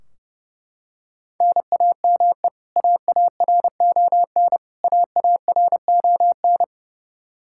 Morse code, heard as a keyed tone, about 30 wpm, 700 Hz. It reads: NAME AARON AARON
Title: Practice QSO - Exchange 1 - Beginning - 30wpm